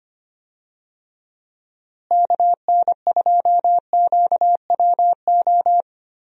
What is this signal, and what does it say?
Morse code, heard as a keyed tone, about 25 wpm, 700 Hz. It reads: KN2QWO
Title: USA Call Signs - 6 Characters - Repeated 2x - 25wpm